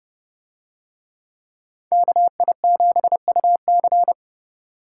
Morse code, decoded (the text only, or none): KI7UC